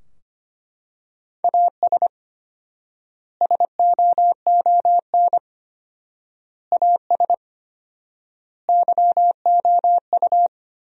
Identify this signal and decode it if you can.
Morse code, heard as a keyed tone, about 25 wpm, 700 Hz. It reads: AS SOON AS YOU
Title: 4x Word-Spacing - Sentences from Top 400 Words - 25wpm